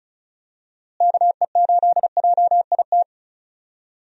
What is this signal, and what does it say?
Morse code, heard as a keyed tone, about 36 wpm, 700 Hz. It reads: KE8JIT
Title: USA Call Signs - 6 Characters - Repeated 2x - 35wpm